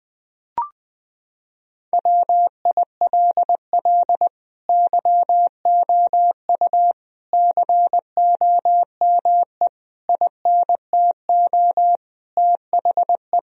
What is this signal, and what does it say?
Morse code, heard as a keyed tone, about 20 wpm, 700 Hz. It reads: WILL YOU COME INTO THE
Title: Sentences from Top 1000 Words - Review Part 4 - 20wpm